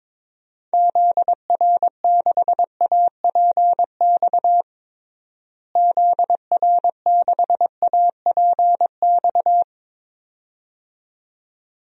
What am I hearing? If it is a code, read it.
Morse code, heard as a keyed tone, about 22 wpm, 700 Hz. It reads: ZR6APX ZR6APX
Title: DXCC-100 Call Signs - 6 Characters - Repeated 2x - 22wpm